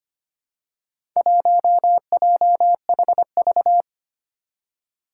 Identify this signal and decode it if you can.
Morse code, heard as a keyed tone, about 25 wpm, 700 Hz. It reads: 1JHV
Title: Four Challenging Letter-Number Combinations 25wpm